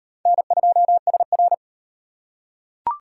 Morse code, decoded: N2SR

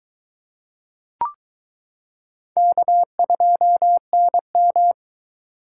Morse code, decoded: K2NM

